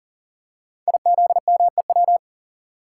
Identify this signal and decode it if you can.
Morse code, heard as a keyed tone, about 42 wpm, 690 Hz. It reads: IZMEW
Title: Code Groups - 5 Letters - 40wpm